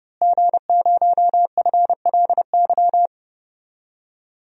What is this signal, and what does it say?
Morse code, heard as a keyed tone, about 30 wpm, 700 Hz. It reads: G0FLY
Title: DXCC-100 Call Signs - 5 Characters - Repeated 3x - 30wpm